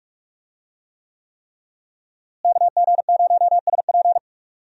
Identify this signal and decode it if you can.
Morse code, heard as a keyed tone, about 46 wpm, 700 Hz. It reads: KG0SP
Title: USA Call Signs - 5 Characters - Repeated 3x - 45wpm